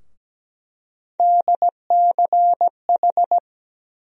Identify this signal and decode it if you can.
Morse code, heard as a keyed tone, about 17 wpm, 700 Hz. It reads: DCH
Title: USA Call Sign Postfixes - 3 Letters 17wpm